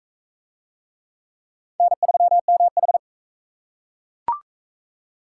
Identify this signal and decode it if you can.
Morse code, heard as a keyed tone, about 40 wpm, 700 Hz. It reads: N3MH